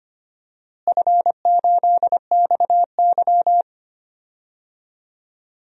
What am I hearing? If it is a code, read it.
Morse code, heard as a keyed tone, about 25 wpm, 700 Hz. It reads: F8XY
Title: DXCC-100 Call Signs - 4 Characters - Repeated 2x - 25wpm